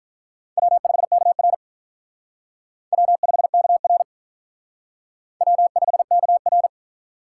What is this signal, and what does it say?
Morse code, heard as a keyed tone, about 46 wpm, 700 Hz. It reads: W5KR W5KR W5KR